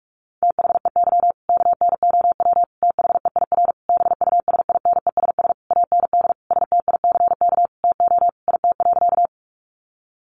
Morse code, heard as a keyed tone, about 44 wpm, 700 Hz. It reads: THEY KNOW THEIR BUSINESS AND STICK TO IT.
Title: Sentences from Top 1200 Words 45wpm